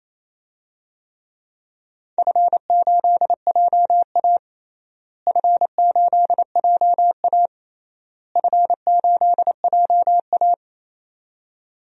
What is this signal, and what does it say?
Morse code, heard as a keyed tone, about 28 wpm, 700 Hz. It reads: F8JA F8JA F8JA
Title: DXCC-100 Call Signs - 4 Characters - Repeated 3x - 28wpm